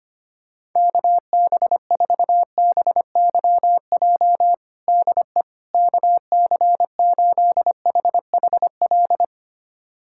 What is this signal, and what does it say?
Morse code, heard as a keyed tone, about 25 wpm, 700 Hz. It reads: KB4BYJ DE KC8HHL
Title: USA Callsign DE Callsign 25wpm